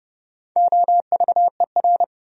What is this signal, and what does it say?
Morse code, heard as a keyed tone, about 32 wpm, 700 Hz. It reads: OVER
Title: Top 100 Words 30wpm